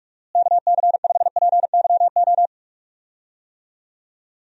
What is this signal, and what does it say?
Morse code, heard as a keyed tone, about 46 wpm, 700 Hz. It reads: KC5PYO
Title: USA Call Signs - 6 Characters - Repeated 3x - 45wpm